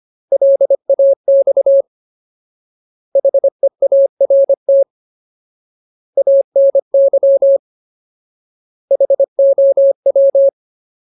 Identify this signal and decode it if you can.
Morse code, heard as a keyed tone, about 25 wpm, 550 Hz. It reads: LAX HEART ANY HOW
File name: Ultimate Word List / 25WPM Ultimate